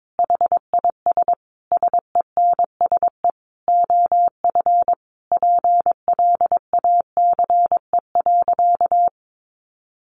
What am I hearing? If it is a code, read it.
Morse code, heard as a keyed tone, about 22 wpm, 700 Hz. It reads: HIS SENSE OF PLACE.
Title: Sentences from Top 1400 Words 22wpm